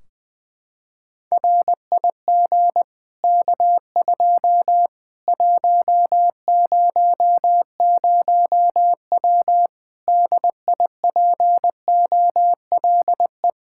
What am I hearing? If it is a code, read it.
Morse code, heard as a keyed tone, about 20 wpm, 700 Hz. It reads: RIG K2 100W DIPOLE
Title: Practice QSO - Exchange 2 - Info - 20wpm